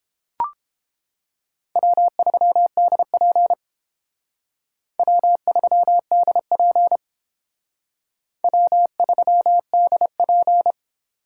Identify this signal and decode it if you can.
Morse code, heard as a keyed tone, about 31 wpm, 700 Hz. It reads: W3DP W3DP W3DP